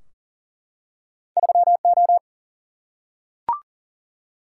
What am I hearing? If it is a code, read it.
Morse code, heard as a keyed tone, about 40 wpm, 700 Hz. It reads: 3O
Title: Single-Letter Single-Number 40wpm